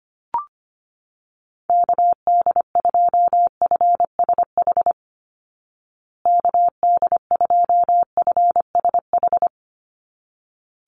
Morse code, heard as a keyed tone, about 25 wpm, 700 Hz. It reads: KD2FSH KD2FSH
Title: Call Signs Repeated 3x - 25wpm